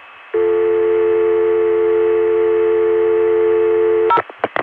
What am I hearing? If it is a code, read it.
Dial tone (North America), 350 + 440 Hz.